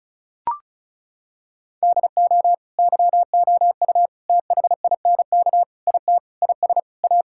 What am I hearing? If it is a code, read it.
Morse code, heard as a keyed tone, about 36 wpm, 700 Hz. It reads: DO YOU THINK IT IS A
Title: Sentences from Top 1000 Words - Review Part 6 - 35wpm